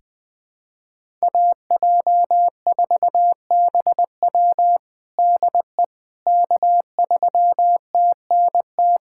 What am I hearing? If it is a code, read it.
Morse code, heard as a keyed tone, about 20 wpm, 700 Hz. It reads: AJ4BW DE K3TNT
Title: USA Callsign DE Callsign - 5 Characters - 20wpm